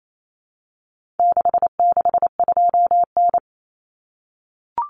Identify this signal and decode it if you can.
Morse code, heard as a keyed tone, about 28 wpm, 700 Hz. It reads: 662N
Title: Common Cut Numbers - 4-Digit 28wpm